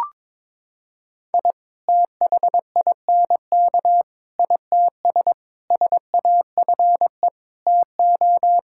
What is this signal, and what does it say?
Morse code, heard as a keyed tone, about 22 wpm, 700 Hz. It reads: I THINK ITS SAFE TO
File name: practice / 22wpm / Sets of 6 Words - 22wpm